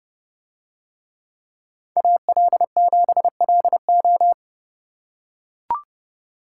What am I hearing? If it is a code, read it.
Morse code, heard as a keyed tone, about 28 wpm, 700 Hz. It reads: AL7LO